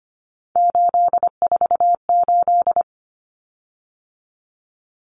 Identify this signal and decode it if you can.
Morse code, heard as a keyed tone, about 25 wpm, 700 Hz. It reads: 848
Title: Common Cut Numbers - 3-Digit 25wpm